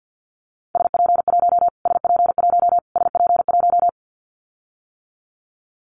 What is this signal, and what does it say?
Morse code, heard as a keyed tone, about 50 wpm, 700 Hz. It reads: SP1 SP1 SP1
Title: Callsign Prefixes for Top 25 DXCC - Repeated 3x - 50wpm